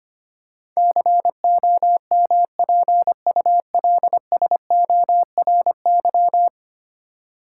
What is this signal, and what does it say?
Morse code, heard as a keyed tone, about 25 wpm, 700 Hz. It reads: COMPULSORY
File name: 10 Letter Words - First 3 Letters Given - 25wpm